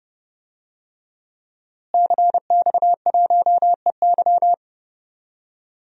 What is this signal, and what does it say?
Morse code, heard as a keyed tone, about 31 wpm, 700 Hz. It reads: CX1EY
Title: DXCC-100 Call Signs - 5 Characters - Repeated 3x - 30wpm